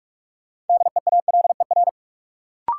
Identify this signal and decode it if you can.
Morse code, heard as a keyed tone, about 47 wpm, 700 Hz. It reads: DEALER
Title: Sentences from Top 1100 Words - New Word Drill Alt - 45wpm